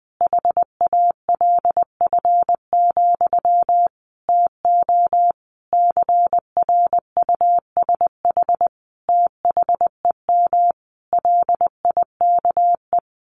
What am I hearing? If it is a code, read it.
Morse code, heard as a keyed tone, about 20 wpm, 700 Hz. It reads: HALF, TO CRUSH THEM LIKE
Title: A Sound Of Thunder 20wpm